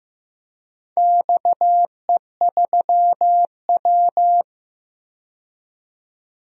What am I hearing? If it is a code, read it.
Morse code, heard as a keyed tone, about 15 wpm, 700 Hz. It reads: XE3W
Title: DXCC-100 Call Signs - 4 Characters - 15wpm